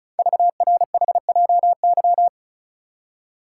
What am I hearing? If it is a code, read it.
Morse code, heard as a keyed tone, about 36 wpm, 690 Hz. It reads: VRHJY
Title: Code Groups - 5 Letter:Numbers - 35wpm